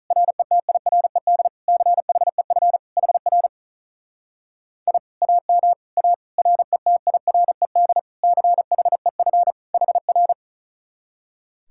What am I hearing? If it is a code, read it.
Morse code, heard as a keyed tone, about 44 wpm, 700 Hz. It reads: RETIRED CHEF HR I AM A RETIRED CHEF HR